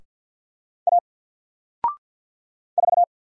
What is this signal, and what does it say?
Morse code, heard as a keyed tone, about 50 wpm, 700 Hz. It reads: A 4
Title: Single Character (Letter:Number:Punctuation:Prosign) 50wpm